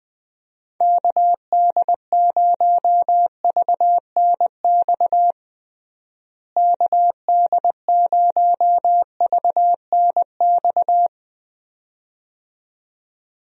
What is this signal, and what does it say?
Morse code, heard as a keyed tone, about 21 wpm, 700 Hz. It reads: KD0VNX KD0VNX
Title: USA Call Signs - 6 Characters - Repeated 2x - 20wpm